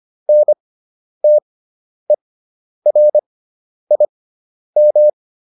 Morse code, heard as a keyed tone, about 25 wpm, spaced as for 10 wpm, 600 Hz. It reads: NTERIM